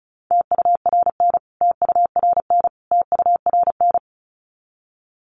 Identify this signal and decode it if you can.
Morse code, heard as a keyed tone, about 34 wpm, 700 Hz. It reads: TURN TURN TURN
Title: Top 200 Words Repeated 35wpm